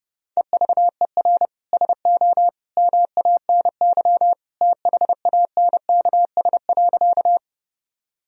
Morse code, heard as a keyed tone, about 30 wpm, 700 Hz. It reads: EVER SO MANY THANKS.
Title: Sentences from Top 1000 Words - Review Part 3 - 30wpm